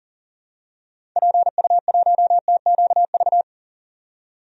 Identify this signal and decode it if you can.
Morse code, heard as a keyed tone, about 40 wpm, 700 Hz. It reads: PU1TQV
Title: DXCC-100 Call Signs - 6 Characters - 40wpm